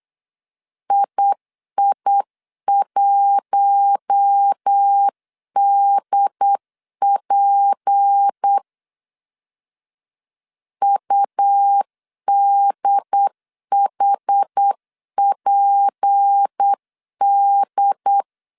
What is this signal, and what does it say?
Morse code, heard as a keyed tone, about 8 wpm, 800 Hz. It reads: II1DP UDHPD
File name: CW / DET / 1 6 1500C S C 8wpm 5219